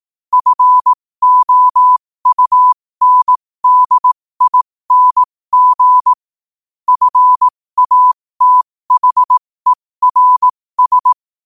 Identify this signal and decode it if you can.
Morse code, heard as a keyed tone, about 18 wpm, spaced as for 15 wpm, 1000 Hz. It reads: FOUNDING FATHERS
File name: qotdfiles / 15wpm